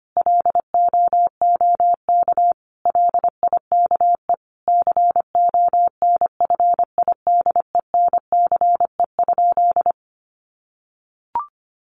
Morse code, heard as a keyed tone, about 25 wpm, 700 Hz. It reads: LOOK LIKE CONFIDENCE?
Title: Sentences from Top 1500 Words 25wpm